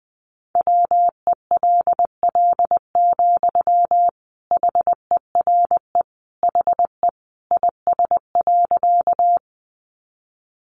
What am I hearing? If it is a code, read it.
Morse code, heard as a keyed tone, about 20 wpm, 700 Hz. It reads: WELL, HERE HE IS.
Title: Sentences from Top 200 Words 20wpm